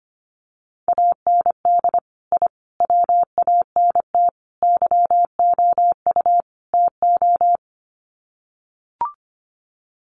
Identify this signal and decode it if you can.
Morse code, heard as a keyed tone, about 25 wpm, 700 Hz. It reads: AND I WANT YOU TO
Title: Sets of 5 Words Encore 25wpm